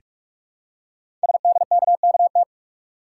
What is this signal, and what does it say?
Morse code, heard as a keyed tone, about 44 wpm, 700 Hz. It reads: SDKKT
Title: Code Groups - 5 Letters - 45wpm